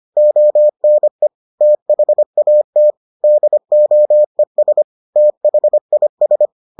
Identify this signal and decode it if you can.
Morse code, heard as a keyed tone, about 25 wpm, 600 Hz. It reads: ONE THAT DOES THIS